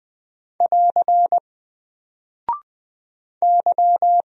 Morse code, heard as a keyed tone, about 20 wpm, 700 Hz. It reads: <AR> Y
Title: Single Character (Letter:Number:Punctuation:Prosign) 20wpm